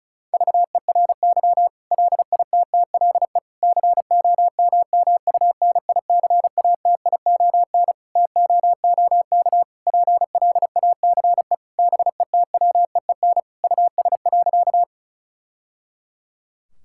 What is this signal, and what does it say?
Morse code, heard as a keyed tone, about 36 wpm, 700 Hz. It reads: VERY LITTLE COMMUNICATION TOOK PLACE BETWEEN US.
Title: Sentences from Top 1000 Words - Review Part 6 - 35wpm